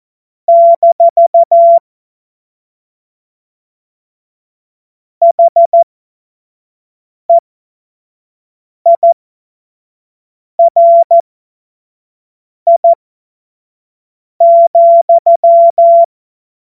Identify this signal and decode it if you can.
Morse code, heard as a keyed tone, about 14 wpm, spaced as for 5 wpm, 680 Hz. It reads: - HEIRI,
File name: Witz-26-August-2025-25-F70